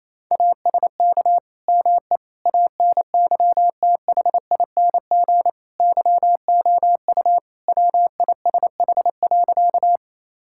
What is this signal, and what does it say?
Morse code, heard as a keyed tone, about 28 wpm, 700 Hz. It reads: ASK ME ANYTHING YOU WISH.
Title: Sentences from Top 1600 Words 28wpm